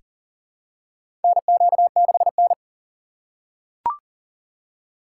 Morse code, heard as a keyed tone, about 40 wpm, 700 Hz. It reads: NQ6N
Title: USA Call Signs - 3x Character Spacing - 40wpm